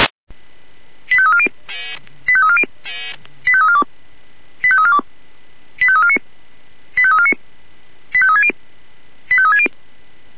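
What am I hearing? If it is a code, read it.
Five-tone selective call (ZVEI), heard as quick runs of five tones, tones of about 70 ms. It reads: 85340 85340 85342 85342 85349 85349 85360 85360
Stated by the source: ZVEI five-tone selective call received on SDR, from feuerwehr-orlamuende.org (fire brigade Orlamünde).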